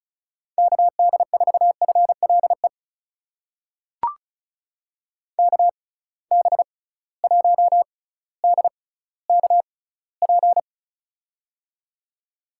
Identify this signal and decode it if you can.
Morse code, heard as a keyed tone, about 36 wpm, spaced as for 18 wpm, 700 Hz. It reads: KD4FLE KB1DKP
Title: USA Call Signs - 6x Character Spacing - 35wpm